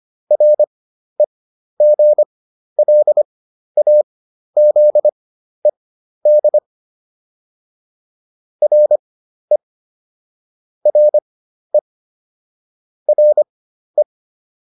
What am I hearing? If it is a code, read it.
Morse code, heard as a keyed tone, about 25 wpm, spaced as for 12 wpm, 600 Hz. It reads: REGLAZED RE RE RE
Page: fundamental-practice-files